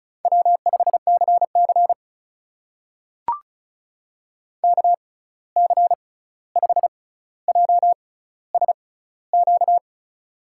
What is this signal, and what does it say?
Morse code, heard as a keyed tone, about 36 wpm, spaced as for 16 wpm, 700 Hz. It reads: W5CC KC5JSQ
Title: USA Call Signs - 6x Character Spacing - 35wpm